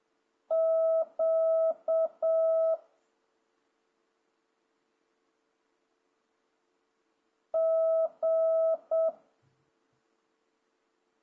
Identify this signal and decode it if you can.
Morse code, heard as a keyed tone, about 7 wpm, 650 Hz. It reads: Q G
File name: jh8cbh / 2026cw